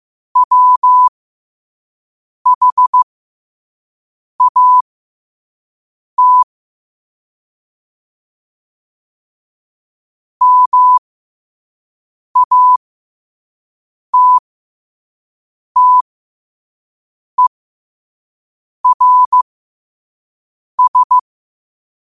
Morse code, heard as a keyed tone, about 15 wpm, spaced as for 5 wpm, 1000 Hz. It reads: WHAT MATTERS
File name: qotdfiles / 5wpm